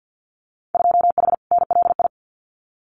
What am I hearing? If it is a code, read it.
Morse code, heard as a keyed tone, about 50 wpm, 700 Hz. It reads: 2H NLI
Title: Winter Field Day Exchanges 50wpm